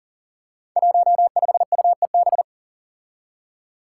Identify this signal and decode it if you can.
Morse code, heard as a keyed tone, about 40 wpm, 700 Hz. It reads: 15UEB
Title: Five Challenging Letter-Number Combinations 40wpm